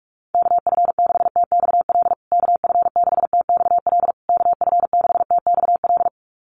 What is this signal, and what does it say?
Morse code, heard as a keyed tone, about 44 wpm, 700 Hz. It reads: KF6TXL KF6TXL KF6TXL
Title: USA Call Signs Repeated 3x 45wpm